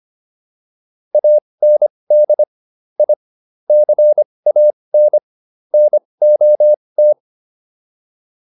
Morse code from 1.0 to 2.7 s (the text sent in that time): AND